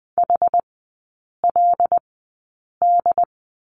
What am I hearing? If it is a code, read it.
Morse code, heard as a keyed tone, about 20 wpm, spaced as for 8 wpm, 690 Hz. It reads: HLD
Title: 081 Lesson 14e - Number 2 - Three Characters 20wpm